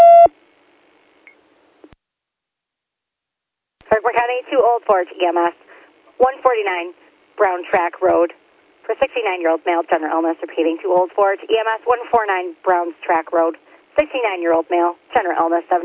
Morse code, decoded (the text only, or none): none